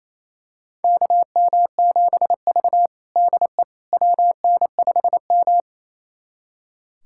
Morse code, heard as a keyed tone, about 28 wpm, 700 Hz. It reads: KM7V DE WN5M
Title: USA Callsign DE Callsign - 4 Characters - 28wpm